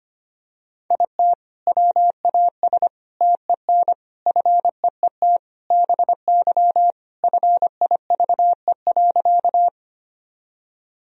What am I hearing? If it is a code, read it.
Morse code, heard as a keyed tone, about 25 wpm, 700 Hz. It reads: IT WAS TEN FEET BY FIVE.
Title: Sentences from Top 1000 Words - Review Part 2 - 25wpm